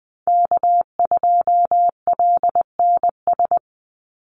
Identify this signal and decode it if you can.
Morse code, heard as a keyed tone, about 20 wpm, 700 Hz. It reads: K2LNS
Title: Call Signs - 20wpm